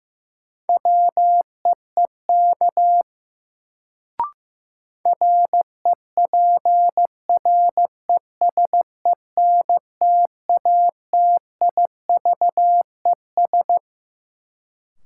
Morse code, heard as a keyed tone, about 15 wpm, 700 Hz. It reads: WEEK REPRESENTATIVES
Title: Sets of 5 Words Encore - New Word Drill - 15wpm